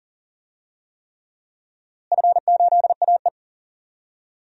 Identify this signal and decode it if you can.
Morse code, heard as a keyed tone, about 39 wpm, 700 Hz. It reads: F8AE
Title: DXCC-100 Call Signs - 4 Characters - Repeated 3x - 40wpm